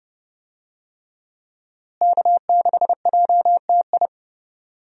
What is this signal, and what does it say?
Morse code, heard as a keyed tone, about 29 wpm, 700 Hz. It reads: K6JTI
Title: USA Call Signs - 5 Characters - Repeated 3x - 30wpm